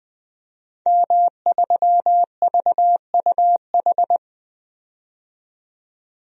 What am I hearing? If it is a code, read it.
Morse code, heard as a keyed tone, about 20 wpm, 700 Hz. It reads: M3VUH
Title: DXCC-100 Call Signs - 5 Characters - 20wpm